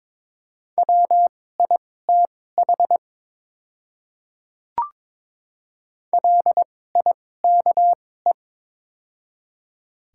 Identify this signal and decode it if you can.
Morse code, heard as a keyed tone, about 22 wpm, spaced as for 16 wpm, 700 Hz. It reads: WITH LIKE
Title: 2x Character Spacing - Top 100 Words - 22wpm